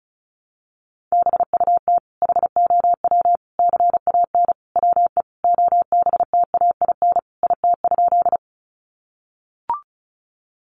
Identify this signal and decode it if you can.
Morse code, heard as a keyed tone, about 34 wpm, 700 Hz. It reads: BUT HOW CAN WE OBTAIN IT?
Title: Sentences from Top 1300 Words 35wpm